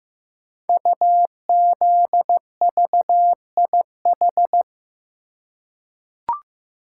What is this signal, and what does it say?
Morse code, heard as a keyed tone, about 15 wpm, 700 Hz. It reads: UZVIH
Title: Code Groups - 5 Letters - 15wpm